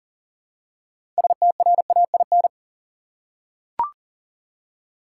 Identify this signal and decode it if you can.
Morse code, heard as a keyed tone, about 37 wpm, 700 Hz. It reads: STRAIN
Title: Six Letter Words - 40wpm